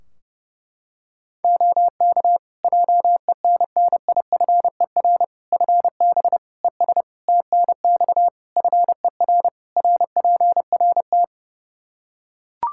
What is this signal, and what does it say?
Morse code, heard as a keyed tone, about 32 wpm, 700 Hz. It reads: OK JENNIFER FB ES TNX FER RPRT
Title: Practice QSO - Exchange 2 - Info - 30wpm